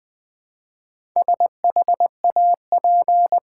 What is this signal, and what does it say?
Morse code, heard as a keyed tone, about 20 wpm, 700 Hz. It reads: SHAP
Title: Build A Word - 1 Letter At A Time - 3 to 5-Letter Words - 20wpm